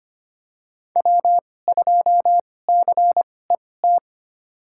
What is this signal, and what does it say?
Morse code, heard as a keyed tone, about 25 wpm, spaced as for 18 wpm, 700 Hz. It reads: W2CET
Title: USA Call Signs - 5 Characters - 2x Character Spacing - 25wpm